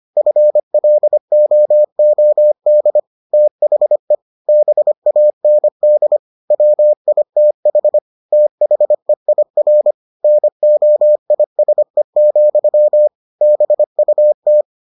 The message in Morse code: FLOOD THE BAND WITH THEIR NOISE, BUT

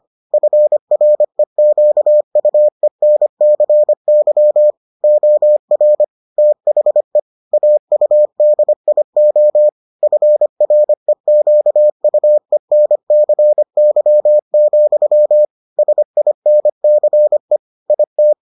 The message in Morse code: FREQUENCY OR THE AUDIO FREQUENCY, SINCE IT

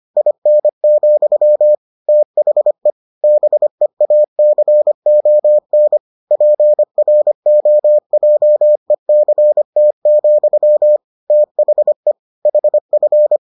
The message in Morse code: IN, THE BEACON PROJECT, THE HF

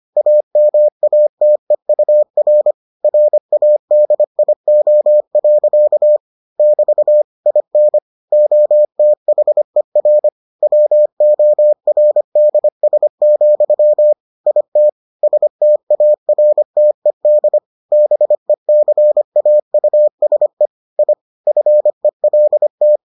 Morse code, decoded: AMATEUR RADIO. <BT> IN OTHER WORDS, IT STARTED BECAUSE I FELT

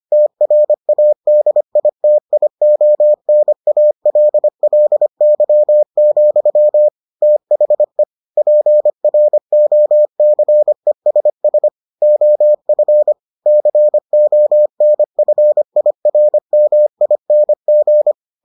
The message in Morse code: TRADITIONALLY, THE PROCESS OF CONFIRMING

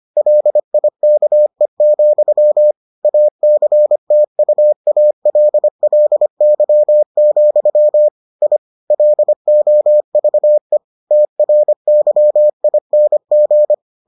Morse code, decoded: LIKE, ACTUALLY, I LOVE TRYING

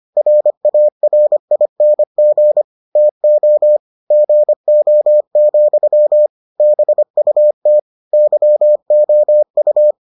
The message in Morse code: RARING TO GO, BUT YOU

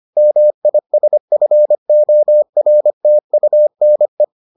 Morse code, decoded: MISFORTUNE

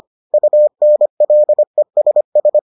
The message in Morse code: UNLESS

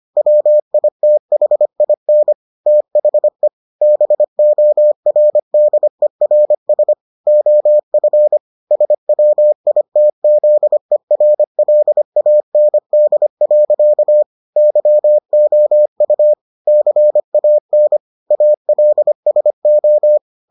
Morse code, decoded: WITHIN THE BORDERS OF SWITZERLAND. YOU CAN ALSO

